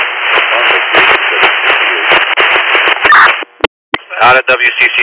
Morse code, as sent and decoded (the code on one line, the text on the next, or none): none